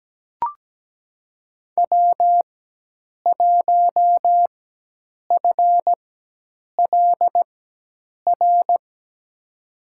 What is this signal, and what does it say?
Morse code, heard as a keyed tone, about 17 wpm, spaced as for 8 wpm, 700 Hz. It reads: W1FLR
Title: USA Call Signs - 5 Characters - 4x Character Spacing - 17wpm